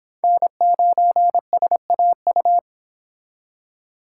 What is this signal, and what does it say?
Morse code, heard as a keyed tone, about 26 wpm, 700 Hz. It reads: N9SAU